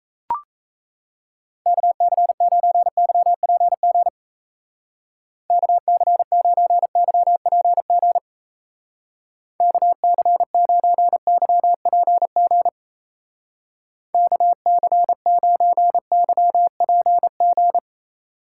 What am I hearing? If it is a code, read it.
Morse code, heard as a keyed tone, about 34 wpm, 700 Hz. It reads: KC9YPG KC9YPG KC9YPG KC9YPG
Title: Speed Racing - USA 2x3 Call Signs 28wpm